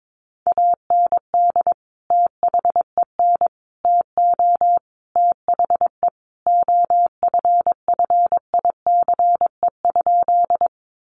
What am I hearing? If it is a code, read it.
Morse code, heard as a keyed tone, about 22 wpm, 700 Hz. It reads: AND THEN TO THE OFFICE?
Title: Sentences from Top 1000 Words 22wpm